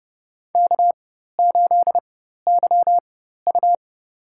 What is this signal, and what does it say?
Morse code, heard as a keyed tone, about 30 wpm, spaced as for 14 wpm, 700 Hz. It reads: K8YU
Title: USA Call Signs - 4 Characters - 4x Character Spacing - 30wpm